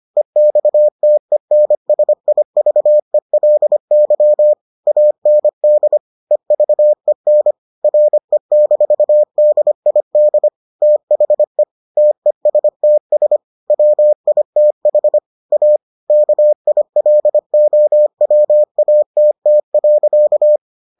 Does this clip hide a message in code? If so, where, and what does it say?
Yes: Morse code from 0.2 to 20.6 s, heard as a keyed tone, about 25 wpm, 600 Hz. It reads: EXTENSIVELY AND EVEN RE-DID THE TESTS WITH A KILOWATT.